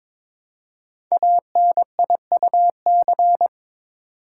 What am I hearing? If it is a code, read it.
Morse code, heard as a keyed tone, about 22 wpm, 700 Hz. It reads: ANIUC
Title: Code Groups - 5 Letter:Numbers - 22wpm